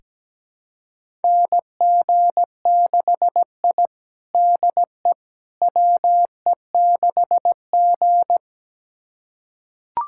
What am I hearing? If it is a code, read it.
Morse code, heard as a keyed tone, about 17 wpm, 700 Hz. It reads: NG6I DE WE6G
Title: USA Callsign DE Callsign - 4 Characters - 17wpm